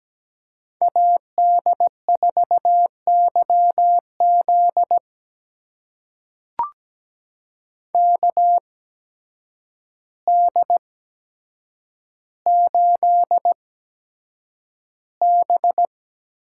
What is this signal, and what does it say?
Morse code, heard as a keyed tone, about 17 wpm, spaced as for 8 wpm, 700 Hz. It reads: AD4YZ KD8B
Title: USA Call Signs - 5 Characters - 8x Character Spacing - 17wpm